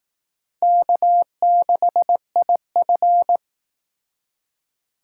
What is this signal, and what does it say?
Morse code, heard as a keyed tone, about 18 wpm, 700 Hz. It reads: K6IF
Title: Speed Racing - USA Call Signs 15wpm